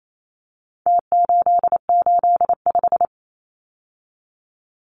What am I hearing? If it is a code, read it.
Morse code, heard as a keyed tone, about 28 wpm, 700 Hz. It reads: T885
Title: Numbers - Contest Style - 4-Digit 28wpm